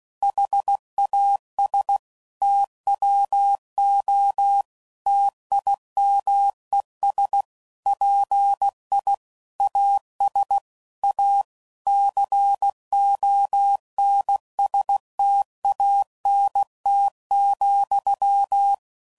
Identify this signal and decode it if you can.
Morse code, heard as a keyed tone, about 16 wpm, 800 Hz. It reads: HAS TWO TIMES PI AS A CONSTANT,
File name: CW Code Groups @15WPM / Technical Subjects / XC